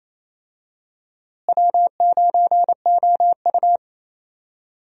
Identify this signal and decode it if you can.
Morse code, heard as a keyed tone, about 28 wpm, 700 Hz. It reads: W9OU
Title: USA Call Signs - 4 Characters - Repeated 2x - 28wpm